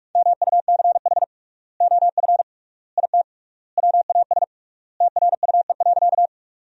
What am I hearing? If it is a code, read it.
Morse code, heard as a keyed tone, about 43 wpm, 700 Hz. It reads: MUCH OF IT WAS TRUE.
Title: 3x Word-Spacing - Sentences from Top 400 Words - 45wpm